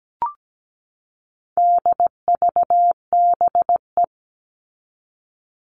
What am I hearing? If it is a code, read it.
Morse code, heard as a keyed tone, about 17 wpm, 700 Hz. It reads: DVBE
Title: All Cut Numbers - 4-Digit 17wpm